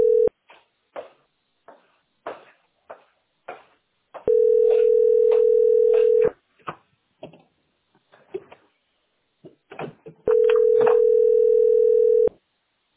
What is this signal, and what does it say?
Ringback tone (North America), 440 + 480 Hz.